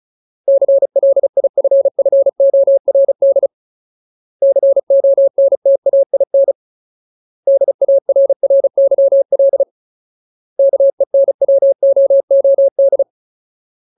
Morse code, heard as a keyed tone, about 34 wpm, 550 Hz. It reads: CLIFFORD CONTAIN DARRYL KENWOOD
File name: Ultimate Word List / 35WPM Ultimate